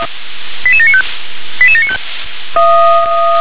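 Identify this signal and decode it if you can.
Five-tone selective call (ZVEI), heard as quick runs of five tones, tones of about 70 ms. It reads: 88795 88795
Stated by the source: Received on SDR, ZVEI five-tone code 88793, 357 channel.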